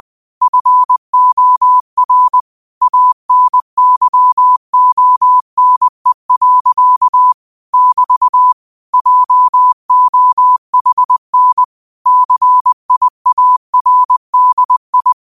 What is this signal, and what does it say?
Morse code, heard as a keyed tone, about 20 wpm, 1000 Hz. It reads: FOR ANYONE. <BT> JOHN CIARDI
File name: qotdfiles / 20wpm